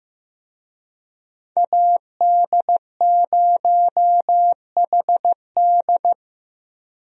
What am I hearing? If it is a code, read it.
Morse code, heard as a keyed tone, about 15 wpm, 700 Hz. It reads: AD0HD
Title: USA Call Signs - 5 Characters - 15wpm